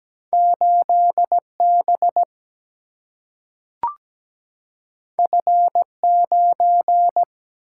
Morse code, heard as a keyed tone, about 17 wpm, 700 Hz. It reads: 8B F9
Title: Single-Letter Single-Number 17wpm